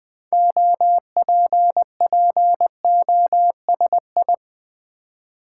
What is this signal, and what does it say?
Morse code, heard as a keyed tone, about 20 wpm, 700 Hz. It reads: OPPOSI
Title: Build A Word - 3 Letters At A Time - 8-Letter Words - 20wpm